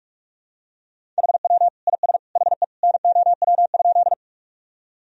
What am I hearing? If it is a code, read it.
Morse code, heard as a keyed tone, about 46 wpm, 700 Hz. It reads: HW IS HE NOW?
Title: Sentences from Top 100 Words and Abbreviations 45wpm